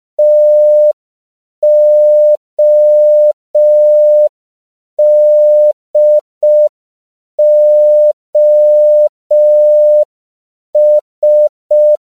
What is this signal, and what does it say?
Morse code, heard as a keyed tone, about 5 wpm, 600 Hz. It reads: TODOS